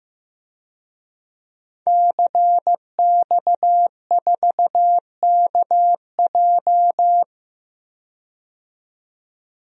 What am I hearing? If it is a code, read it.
Morse code, heard as a keyed tone, about 15 wpm, 700 Hz. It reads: CX4KJ
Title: DXCC-100 Call Signs - 5 Characters - 15wpm